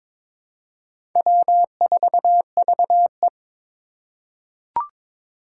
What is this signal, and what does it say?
Morse code, heard as a keyed tone, about 22 wpm, 700 Hz. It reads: W4VE
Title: Four Challenging Letter-Number Combinations 22wpm